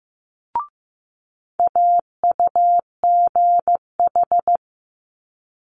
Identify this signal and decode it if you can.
Morse code, heard as a keyed tone, about 15 wpm, 700 Hz. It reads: AUGH
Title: Multi-letter Phonemes Quick Fire 15wpm